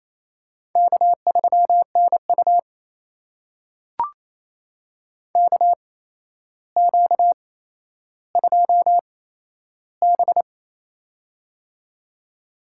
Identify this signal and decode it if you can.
Morse code, heard as a keyed tone, about 28 wpm, spaced as for 12 wpm, 700 Hz. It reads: K3NU KQ2B
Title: USA Call Signs - 4 Characters - 8x Character Spacing - 28wpm